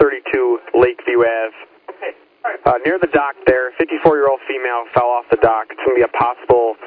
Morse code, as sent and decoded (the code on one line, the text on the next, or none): none